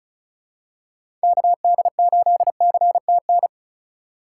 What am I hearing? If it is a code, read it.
Morse code, heard as a keyed tone, about 36 wpm, 700 Hz. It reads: KD8CTN
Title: USA Call Signs - 2x Character Spacing - 35wpm